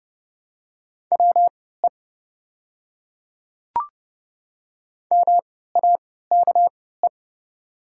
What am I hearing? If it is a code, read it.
Morse code, heard as a keyed tone, about 30 wpm, spaced as for 17 wpm, 700 Hz. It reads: WE MAKE
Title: 3x Character Spacing - Top 100 Words - 30wpm